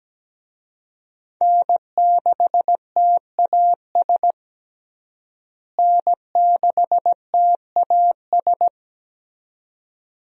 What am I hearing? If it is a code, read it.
Morse code, heard as a keyed tone, about 17 wpm, 700 Hz. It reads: N6TAS N6TAS
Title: USA Call Signs - 5 Characters - Repeated 2x - 17wpm